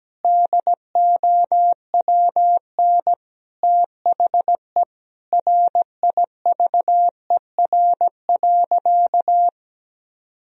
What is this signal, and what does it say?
Morse code, heard as a keyed tone, about 17 wpm, 700 Hz. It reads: DOWN THE RIVER.
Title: Sentences from Top 1000 Words - Review Part 1 - 17wpm